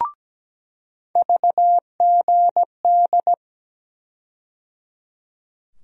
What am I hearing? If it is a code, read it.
Morse code, heard as a keyed tone, about 17 wpm, 700 Hz. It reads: VGD
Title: USA Call Sign Postfixes - 3 Letters 17wpm